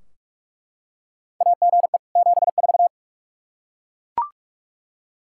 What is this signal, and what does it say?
Morse code, heard as a keyed tone, about 41 wpm, 700 Hz. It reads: AGE 74
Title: Practice QSO - Exchange 3 - More Info - 45wpm